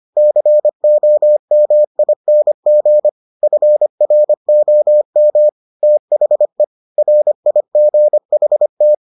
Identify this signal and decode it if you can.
Morse code, heard as a keyed tone, about 25 wpm, 600 Hz. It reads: COMING FROM THE RIGHT